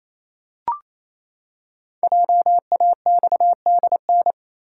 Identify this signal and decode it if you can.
Morse code, heard as a keyed tone, about 27 wpm, 700 Hz. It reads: JAXDN
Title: Code Groups - 5 Letters - 28wpm